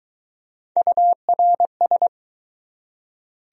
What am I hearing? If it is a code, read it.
Morse code, heard as a keyed tone, about 20 wpm, 700 Hz. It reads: URS